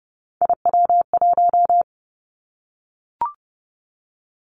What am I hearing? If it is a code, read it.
Morse code, heard as a keyed tone, about 30 wpm, 700 Hz. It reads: IW1
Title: Callsign Prefixes for Top 50 DXCC - 30wpm